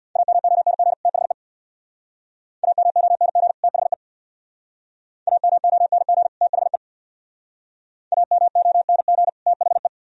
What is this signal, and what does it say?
Morse code, heard as a keyed tone, about 63 wpm, 700 Hz. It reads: AMONG THE AMONG THE AMONG THE AMONG THE